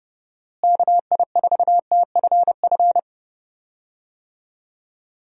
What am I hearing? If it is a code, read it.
Morse code, heard as a keyed tone, about 31 wpm, 700 Hz. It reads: KI4TFF
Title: USA Call Signs - 6 Characters - Repeated 3x - 30wpm